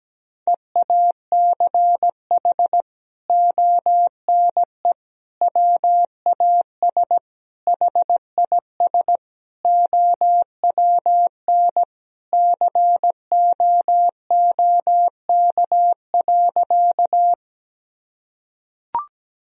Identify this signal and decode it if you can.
Morse code, heard as a keyed tone, about 17 wpm, 700 Hz. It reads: EACH ONE WAS HIS OWN COOK.
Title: Sentences from Top 1000 Words - Review Part 4 - 17wpm